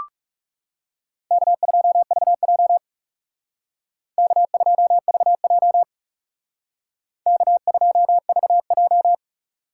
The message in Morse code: K2VJ K2VJ K2VJ

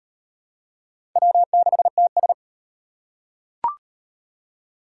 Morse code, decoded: W6TS

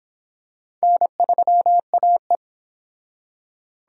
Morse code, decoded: N3AE

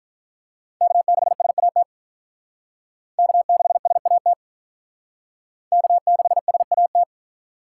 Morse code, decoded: K6SAT K6SAT K6SAT

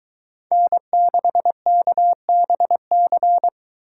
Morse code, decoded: N6KBC

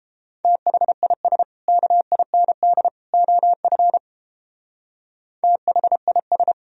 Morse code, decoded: THIS KIND OF THIS

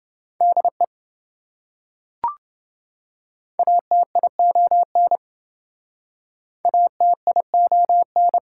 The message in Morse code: DE ATION ATION

